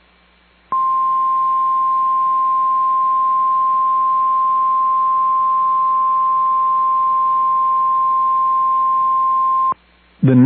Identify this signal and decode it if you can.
Emergency Alert System (SAME) tones (the 1050 Hz attention tone). It is heard from 0.6 to 9.8 s.